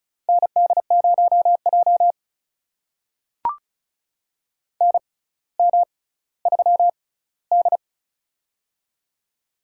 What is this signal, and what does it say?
Morse code, heard as a keyed tone, about 36 wpm, spaced as for 18 wpm, 700 Hz. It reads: ND0J NM3D
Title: USA Call Signs - 4 Characters - 6x Character Spacing - 35wpm